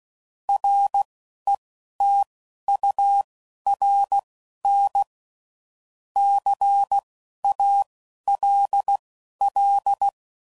Morse code, heard as a keyed tone, about 16 wpm, spaced as for 11 wpm, 800 Hz. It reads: RETURN CALL
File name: CW Code Groups @10WPM / QSOs / QSO kc7sac